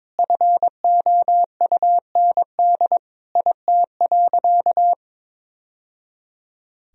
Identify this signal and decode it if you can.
Morse code, heard as a keyed tone, about 22 wpm, 700 Hz. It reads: FOUND IT.
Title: Sentences from Top 1000 Words - Review Part 1 - 22wpm